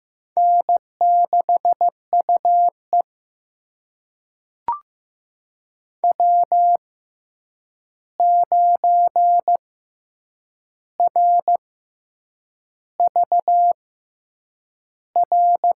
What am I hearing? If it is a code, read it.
Morse code, heard as a keyed tone, about 15 wpm, spaced as for 7 wpm, 700 Hz. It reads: N6UE W9RVR
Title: USA Call Signs - 6x Character Spacing - 15wpm